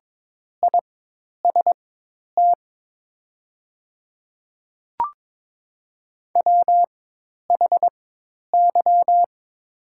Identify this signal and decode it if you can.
Morse code, heard as a keyed tone, about 23 wpm, spaced as for 10 wpm, 700 Hz. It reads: IST WHY
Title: 4x Character Spacing - Top 500 Words - 22wpm